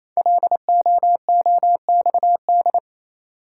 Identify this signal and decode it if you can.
Morse code, heard as a keyed tone, about 27 wpm, 700 Hz. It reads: LOOXD
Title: Code Groups - 5 Letter:Numbers - 28wpm